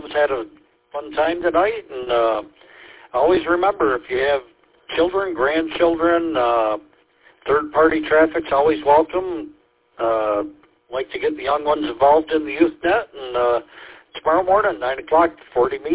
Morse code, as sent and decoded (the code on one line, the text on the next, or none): none